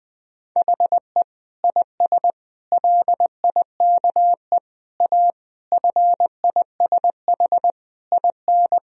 Morse code, heard as a keyed tone, about 19 wpm, 700 Hz. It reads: HE IS LIKE A FISH IN
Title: Sentences from Top 1000 Words - Review Part 1 - 20wpm